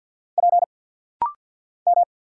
Morse code, heard as a keyed tone, about 48 wpm, 700 Hz. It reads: P M
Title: Single Letter-Number Alt - Rapid-Fire 50wpm